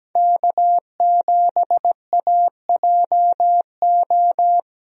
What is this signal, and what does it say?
Morse code, heard as a keyed tone, about 17 wpm, 700 Hz. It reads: K7AJO
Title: USA Call Signs - 5 Characters - Repeated 3x - 17wpm